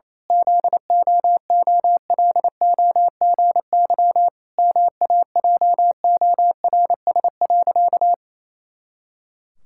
Morse code, heard as a keyed tone, about 28 wpm, 700 Hz. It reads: ZOOLOGY MAJORS.
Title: Pangram Challenge v2 - 28wpm